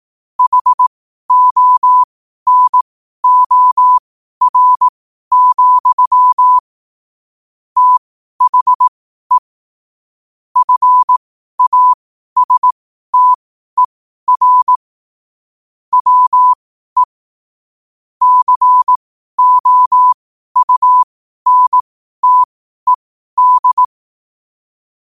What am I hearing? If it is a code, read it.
Morse code, heard as a keyed tone, about 18 wpm, spaced as for 12 wpm, 1000 Hz. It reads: HONOR, THE FASTER WE COUNTED
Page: New quotes every day in morse code at 12 Words per minute.